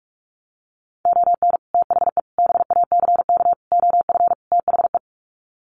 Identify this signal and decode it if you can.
Morse code, heard as a keyed tone, about 44 wpm, 700 Hz. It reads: ON THE BACK OF THE
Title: Sets of 5 Words 45wpm